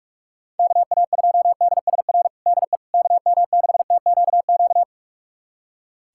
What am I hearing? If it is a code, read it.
Morse code, heard as a keyed tone, about 41 wpm, 700 Hz. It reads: KA2DSR DE KM6TQQ
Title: USA Callsign DE Callsign 45wpm